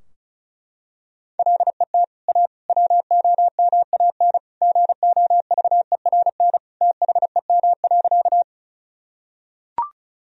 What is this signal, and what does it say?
Morse code, heard as a keyed tone, about 36 wpm, 700 Hz. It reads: LET A WOMAN GOVERN THEM.
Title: Sentences from Top 1000 Words - Review Part 2 - 35wpm